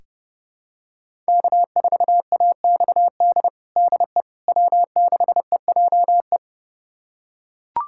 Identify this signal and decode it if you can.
Morse code, heard as a keyed tone, about 30 wpm, 700 Hz. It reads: K4AXD DE W6EJE
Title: USA Callsign DE Callsign 30wpm